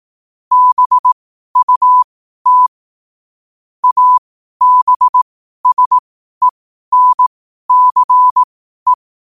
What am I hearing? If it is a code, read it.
Morse code, heard as a keyed tone, about 18 wpm, spaced as for 12 wpm, 1000 Hz. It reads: BUT ABSENCE
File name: qotdfiles / 12wpm